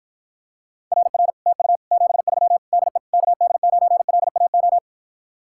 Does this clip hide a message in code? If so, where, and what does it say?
Yes: Morse code from 0.9 to 4.8 s, heard as a keyed tone, about 52 wpm, 700 Hz. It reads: RR TU 73 DE KD9LAO